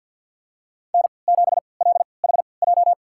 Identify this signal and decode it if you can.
Morse code, heard as a keyed tone, about 47 wpm, spaced as for 28 wpm, 700 Hz. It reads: N7LHJ
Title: USA Call Signs - 5 Characters - 3x Character Spacing - 50wpm